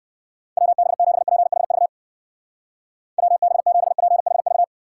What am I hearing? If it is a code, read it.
Morse code, heard as a keyed tone, about 57 wpm, 700 Hz. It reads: WB7PHV WB7PHV